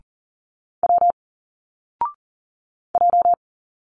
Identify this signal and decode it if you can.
Morse code, heard as a keyed tone, about 40 wpm, 700 Hz. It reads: W J
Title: Single Letters 40wpm